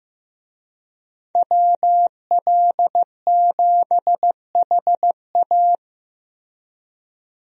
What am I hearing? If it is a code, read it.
Morse code, heard as a keyed tone, about 15 wpm, 700 Hz. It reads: WL7HA
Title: USA Call Signs - 5 Characters - Repeated 2x - 15wpm